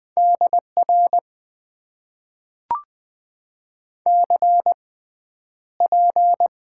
Morse code, heard as a keyed tone, about 20 wpm, spaced as for 10 wpm, 700 Hz. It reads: DR CP
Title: Top 90 QSO Elements - 6x Character Spacing - 20wpm